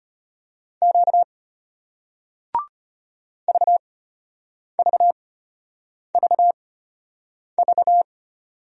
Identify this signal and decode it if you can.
Morse code, heard as a keyed tone, about 34 wpm, spaced as for 8 wpm, 700 Hz. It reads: Q VVVV